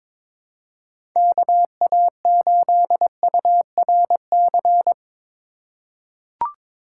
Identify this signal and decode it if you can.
Morse code, heard as a keyed tone, about 22 wpm, 700 Hz. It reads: KA8URC
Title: USA Call Signs - 6 Characters - 3x Character Spacing - 22wpm